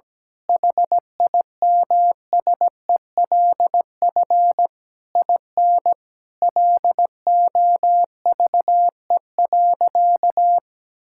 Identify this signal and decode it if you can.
Morse code, heard as a keyed tone, about 17 wpm, 700 Hz. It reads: HIMSELF IN LOVE.
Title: Sentences from Top 1600 Words 17wpm